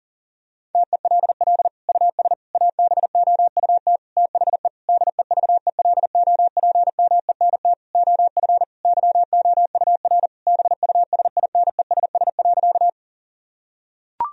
Morse code, heard as a keyed tone, about 40 wpm, 700 Hz. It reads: TELL US ABOUT THE DEVELOPMENT OF YOUR BUSINESS.
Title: Sentences from Top 1000 Words - Review Part 4 - 40wpm